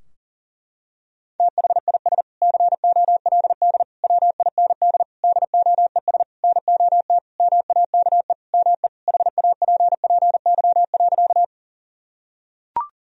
Morse code, heard as a keyed tone, about 40 wpm, 700 Hz. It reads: THIS COLD WIND DOES NOT MAKE ME HAPPY.
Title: Sentences from Top 1000 Words - Review Part 2 - 40wpm